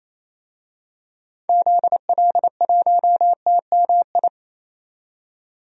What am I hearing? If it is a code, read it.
Morse code, heard as a keyed tone, about 28 wpm, 700 Hz. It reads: ZL1TMI
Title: DXCC-100 Call Signs - 6 Characters - 28wpm